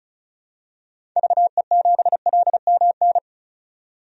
Morse code, heard as a keyed tone, about 36 wpm, 700 Hz. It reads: VE7LMN
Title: DXCC-100 Call Signs - 6 Characters - 35wpm